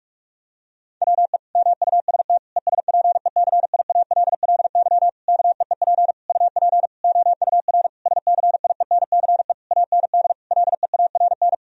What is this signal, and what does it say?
Morse code, heard as a keyed tone, about 46 wpm, 700 Hz. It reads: WE MUST ESPECIALLY KEEP UP OUR SCIENCE AND LEARN
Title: Sentences from Top 1000 Words - Review Part 4 - 45wpm